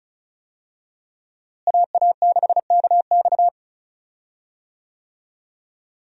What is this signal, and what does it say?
Morse code, heard as a keyed tone, about 34 wpm, 700 Hz. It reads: AA6KX
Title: USA Call Signs - 5 Characters - Repeated 2x - 35wpm